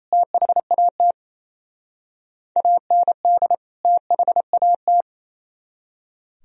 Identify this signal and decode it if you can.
Morse code, heard as a keyed tone, about 32 wpm, 700 Hz. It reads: THAT AND THAT
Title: Speed-Racing - Sets of 2 Words 28wpm